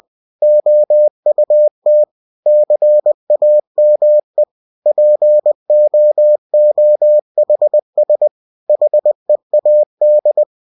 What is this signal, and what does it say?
Morse code, heard as a keyed tone, about 20 wpm, 600 Hz. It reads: OUT CAME POOHS HEAD
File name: cw-books / winnie-the-pooh / winnie-20wpm-page-0043